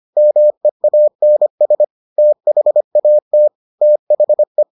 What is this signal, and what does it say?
Morse code, heard as a keyed tone, about 25 wpm, 600 Hz. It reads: MEANS THAT THE